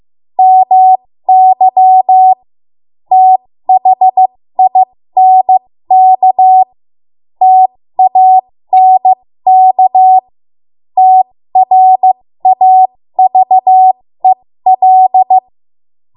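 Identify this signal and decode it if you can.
Morse code, heard as a keyed tone, about 15 wpm, spaced as for 13 wpm, 750 Hz. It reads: MY THINK TANK TRAVEL
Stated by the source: 13 wpm Morse code practice archive transmission.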